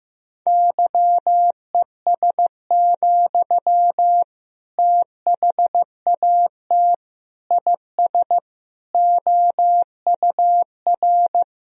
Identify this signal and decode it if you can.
Morse code, heard as a keyed tone, about 15 wpm, 700 Hz. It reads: YES, THAT IS OUR
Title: Sentences from Top 1000 Words - Review Part 3 - 15wpm